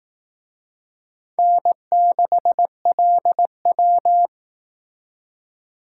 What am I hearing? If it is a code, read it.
Morse code, heard as a keyed tone, about 18 wpm, 700 Hz. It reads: N6LW
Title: Speed Racing - USA 1x2 Call Signs 15wpm